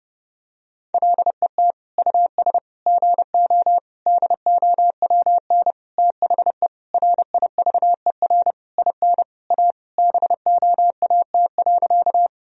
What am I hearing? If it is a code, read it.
Morse code, heard as a keyed tone, about 30 wpm, 700 Hz. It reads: LET US GO DOWN THE RIVER IN A BOAT.
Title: Sentences from Top 1000 Words - Review Part 3 - 30wpm